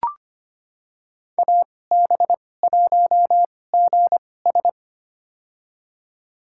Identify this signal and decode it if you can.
Morse code, heard as a keyed tone, about 25 wpm, spaced as for 18 wpm, 700 Hz. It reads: AB1GS
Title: USA Call Signs - 5 Characters - 2x Character Spacing - 25wpm